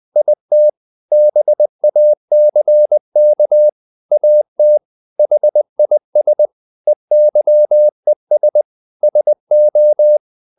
Morse code, heard as a keyed tone, about 20 wpm, 600 Hz. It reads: IT BACK AT HIS EYES SO
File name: the old man and the sea / the old man and the sea part 1 20wpm